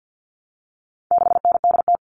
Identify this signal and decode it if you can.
Morse code, heard as a keyed tone, about 48 wpm, 700 Hz. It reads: 6NDT
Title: All Cut Numbers - 4-Digit 50wpm